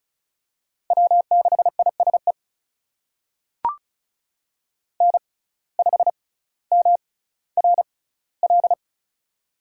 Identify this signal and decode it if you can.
Morse code, heard as a keyed tone, about 36 wpm, spaced as for 18 wpm, 700 Hz. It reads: W6ISE N5MRL
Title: USA Call Signs - 6x Character Spacing - 35wpm